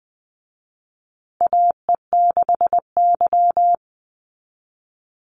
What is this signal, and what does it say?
Morse code, heard as a keyed tone, about 19 wpm, 700 Hz. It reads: AE6Y
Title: Call Signs Repeated 3x - 20wpm